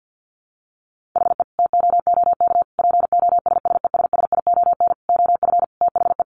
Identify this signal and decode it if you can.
Morse code, heard as a keyed tone, about 50 wpm, 700 Hz. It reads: HE TOOK POSSESSION OF THE
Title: Sentences from Top 1500 Words 50wpm